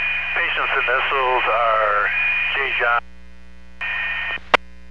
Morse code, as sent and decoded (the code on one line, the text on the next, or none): none